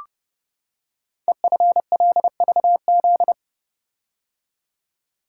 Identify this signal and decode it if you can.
Morse code, heard as a keyed tone, about 30 wpm, 690 Hz. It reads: EFLVZ
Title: Code Groups - 5 Letters - 30wpm